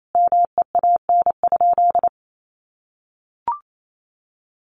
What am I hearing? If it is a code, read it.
Morse code, heard as a keyed tone, about 28 wpm, 700 Hz. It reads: MEAN?
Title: Sentences from Top 200 Words 28wpm